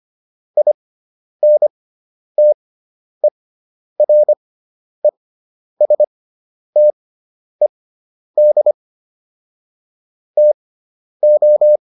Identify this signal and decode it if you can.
Morse code, heard as a keyed tone, about 25 wpm, spaced as for 10 wpm, 600 Hz. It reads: INTERESTED TO
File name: QSO409 10